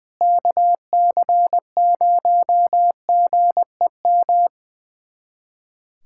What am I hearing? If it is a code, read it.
Morse code, heard as a keyed tone, about 21 wpm, 700 Hz. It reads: KC0GEM
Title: USA Callsign DE Callsign 20wpm